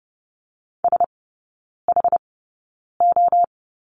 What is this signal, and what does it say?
Morse code, heard as a keyed tone, about 30 wpm, spaced as for 9 wpm, 700 Hz. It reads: SHO
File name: Lesson 18e - Letter m - Three Characters 30wpm